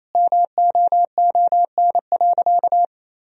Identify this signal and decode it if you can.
Morse code, heard as a keyed tone, about 28 wpm, 700 Hz. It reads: MOON.
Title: Sentences from Top 1000 Words - Review Part 3 - 28wpm